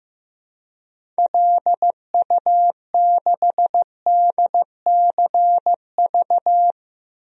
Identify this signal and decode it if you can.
Morse code, heard as a keyed tone, about 15 wpm, 700 Hz. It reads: LU6DCV
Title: DXCC-100 Call Signs - 6 Characters - Repeated 2x - 15wpm